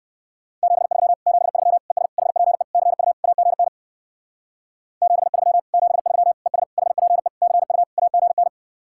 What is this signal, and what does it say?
Morse code, heard as a keyed tone, about 60 wpm, 700 Hz. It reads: 73 73 ES HPE CU AGN 73 73 ES HPE CU AGN